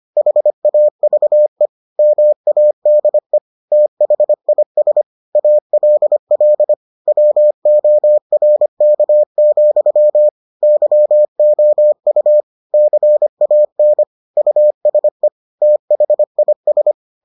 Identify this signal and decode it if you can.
Morse code, heard as a keyed tone, about 25 wpm, 600 Hz. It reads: HAVE MADE THIS ALL WORK, YOU CAN USE THIS